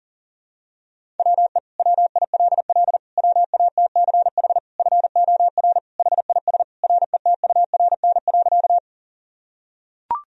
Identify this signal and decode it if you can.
Morse code, heard as a keyed tone, about 40 wpm, 700 Hz. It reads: WE WILL WATCH FOR HIS RETURN.
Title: Sentences from Top 1000 Words - Review Part 1 - 40wpm